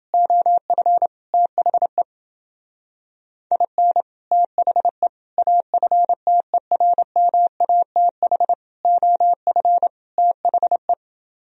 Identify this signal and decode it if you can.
Morse code, heard as a keyed tone, about 28 wpm, 700 Hz. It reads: OF THE IN THE AFTERMATH OF THE